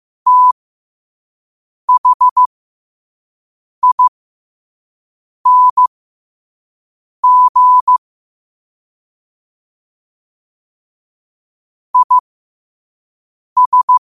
Morse code, heard as a keyed tone, about 15 wpm, spaced as for 5 wpm, 1000 Hz. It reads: THING IS
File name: qotdfiles / 5wpm